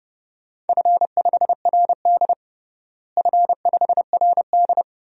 Morse code, heard as a keyed tone, about 30 wpm, 700 Hz. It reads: F5RD F5RD
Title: DXCC-100 Call Signs - 4 Characters - Repeated 2x - 30wpm